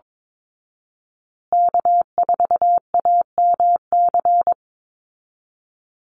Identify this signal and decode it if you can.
Morse code, heard as a keyed tone, about 22 wpm, 700 Hz. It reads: K4AMC
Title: Call Signs Repeated 2x - 22wpm